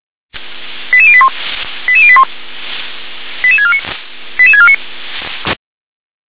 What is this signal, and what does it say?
Five-tone selective call (ZVEI), heard as quick runs of five tones, tones of about 70 ms. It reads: 88081 88081 88649 88649